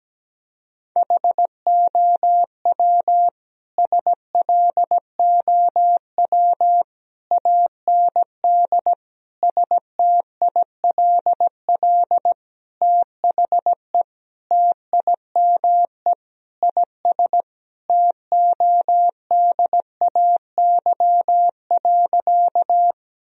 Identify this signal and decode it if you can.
Morse code, heard as a keyed tone, about 17 wpm, 700 Hz. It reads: HOW SLOW AND STILL THE TIME IS TODAY.
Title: Sentences from Top 1000 Words - Review Part 2 - 17wpm